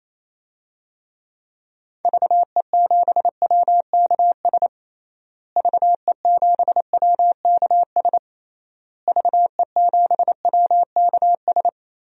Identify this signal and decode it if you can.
Morse code, heard as a keyed tone, about 28 wpm, 700 Hz. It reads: VE7WKS VE7WKS VE7WKS
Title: DXCC-100 Call Signs - 6 Characters - Repeated 3x - 28wpm